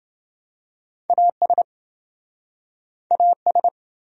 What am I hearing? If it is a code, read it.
Morse code, heard as a keyed tone, about 29 wpm, 700 Hz. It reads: AS AS